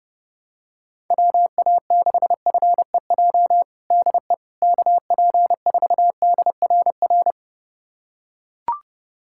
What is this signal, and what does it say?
Morse code, heard as a keyed tone, about 30 wpm, 700 Hz. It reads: WA6FEJ DE KP4DRR
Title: USA Callsign DE Callsign - 6 Characters - 30wpm